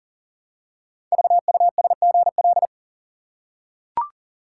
Morse code, heard as a keyed tone, about 42 wpm, 700 Hz. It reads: VUSGL
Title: Code Groups - 5 Letters - 40wpm